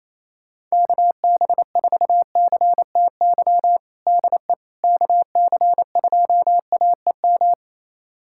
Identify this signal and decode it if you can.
Morse code, heard as a keyed tone, about 28 wpm, 700 Hz. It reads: KB4CTY DE KC2AEM
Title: USA Callsign DE Callsign 28wpm